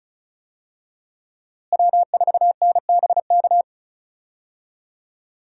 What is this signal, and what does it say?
Morse code, heard as a keyed tone, about 36 wpm, 700 Hz. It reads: W4NBK
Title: USA Call Signs - 5 Characters - Repeated 2x - 35wpm